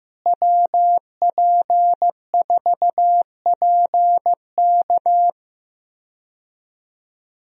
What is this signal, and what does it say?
Morse code, heard as a keyed tone, about 15 wpm, 700 Hz. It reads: WP4PK
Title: USA Call Signs - 5 Characters - Repeated 2x - 15wpm